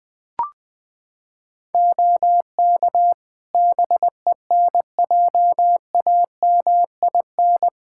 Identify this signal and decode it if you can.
Morse code, heard as a keyed tone, about 20 wpm, 700 Hz. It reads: OK BENJAMIN
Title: K1USN SST QSO Practice 20wpm